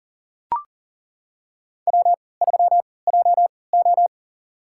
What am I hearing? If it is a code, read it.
Morse code, heard as a keyed tone, about 40 wpm, spaced as for 23 wpm, 700 Hz. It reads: W3JO
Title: USA Call Signs - 4 Characters - 3x Character Spacing - 40wpm